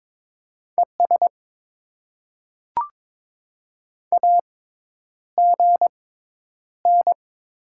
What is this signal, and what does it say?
Morse code, heard as a keyed tone, about 22 wpm, spaced as for 10 wpm, 690 Hz. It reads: ES AGN
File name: Top 20 QSO Elements - 6x Character Spacing - 22wpm